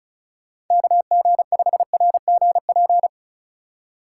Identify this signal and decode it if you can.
Morse code, heard as a keyed tone, about 44 wpm, 700 Hz. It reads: KG5RGP
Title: Speed Racing - USA 2x3 Call Signs 35wpm